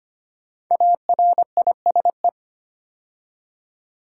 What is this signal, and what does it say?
Morse code, heard as a keyed tone, about 25 wpm, 700 Hz. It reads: ARISE
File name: 5 Letter Words - First 3 Letters Given - 25wpm